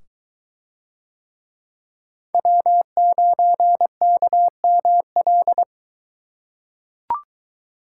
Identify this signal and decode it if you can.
Morse code, heard as a keyed tone, about 20 wpm, 700 Hz. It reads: W9KML